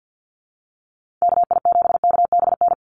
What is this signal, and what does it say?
Morse code, heard as a keyed tone, about 48 wpm, 700 Hz. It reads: KI7KBN
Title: USA Call Signs 50wpm